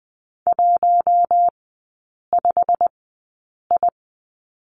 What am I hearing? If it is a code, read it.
Morse code, heard as a keyed tone, about 20 wpm, spaced as for 8 wpm, 700 Hz. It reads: 15I
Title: 087 Lesson 15e - Number 5 - Three Characters 20wpm